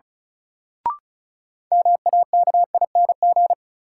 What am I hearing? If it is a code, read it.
Morse code, heard as a keyed tone, about 36 wpm, 700 Hz. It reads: MAKING
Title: Sentences from Top 1000 Words - Review Part 6 - New Word Drill - 35wpm